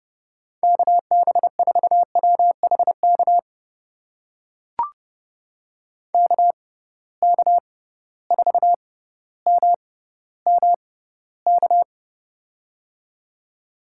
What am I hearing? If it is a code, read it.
Morse code, heard as a keyed tone, about 31 wpm, spaced as for 15 wpm, 700 Hz. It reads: KB4WHK KK4MMK
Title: USA Call Signs - 6 Characters - 6x Character Spacing - 30wpm